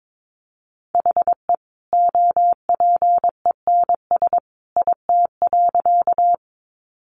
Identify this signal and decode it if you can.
Morse code, heard as a keyed tone, about 22 wpm, 700 Hz. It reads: HE OPENS IT.
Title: Additional Easy Phrases 22wpm